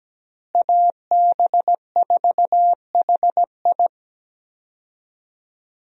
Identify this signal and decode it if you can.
Morse code, heard as a keyed tone, about 17 wpm, 700 Hz. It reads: AB4HI
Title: USA Call Signs - 5 Characters - Repeated 3x - 17wpm